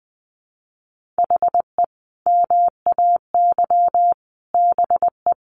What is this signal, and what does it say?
Morse code, heard as a keyed tone, about 20 wpm, 700 Hz. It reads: HE MAY BE
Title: Sentences from Top 1400 Words 20wpm